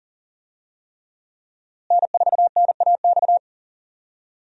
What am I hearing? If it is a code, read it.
Morse code, heard as a keyed tone, about 40 wpm, 700 Hz. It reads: N4NAX
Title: USA Call Signs - 5 Characters - Repeated 3x - 40wpm